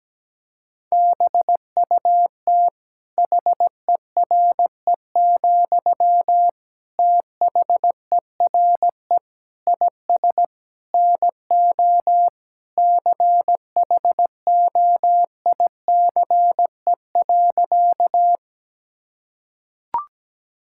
Morse code, heard as a keyed tone, about 17 wpm, 700 Hz. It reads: BUT HERE, THERE IS NO CHOICE.
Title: Sentences from Top 1000 Words - Review Part 6 - 17wpm